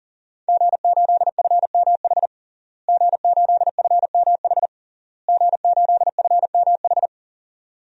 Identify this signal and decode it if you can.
Morse code, heard as a keyed tone, about 40 wpm, 700 Hz. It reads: G8FMH G8FMH G8FMH
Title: DXCC-100 Call Signs - 5 Characters - Repeated 3x - 40wpm